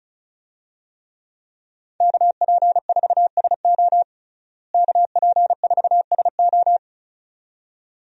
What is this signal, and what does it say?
Morse code, heard as a keyed tone, about 34 wpm, 700 Hz. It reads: KP4SO KP4SO
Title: DXCC-100 Call Signs - 5 Characters - Repeated 2x - 35wpm